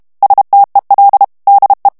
Morse code, heard as a keyed tone, about 32 wpm, 800 Hz. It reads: STEL DE
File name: HOLLANDAIS texte clair 2 vitesse 30 mots minute